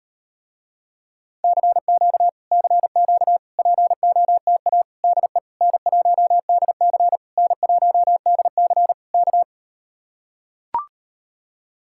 Morse code, heard as a keyed tone, about 39 wpm, 700 Hz. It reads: CQ CQ POTA DE N1DC N1DC K